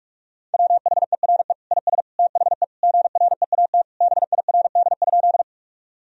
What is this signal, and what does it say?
Morse code, heard as a keyed tone, about 42 wpm, 700 Hz. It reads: WHERE IS THE GREAT BIRD?
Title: Sentences from Top 1000 Words - Review Part 2 - 45wpm